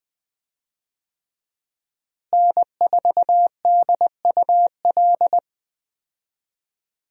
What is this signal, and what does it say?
Morse code, heard as a keyed tone, about 20 wpm, 700 Hz. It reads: N4DUL
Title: USA Call Signs - 5 Characters - Repeated 3x - 20wpm